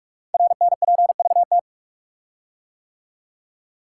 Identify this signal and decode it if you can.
Morse code, heard as a keyed tone, about 44 wpm, 700 Hz. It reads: RNPVT
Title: Code Groups - 5 Letters - 45wpm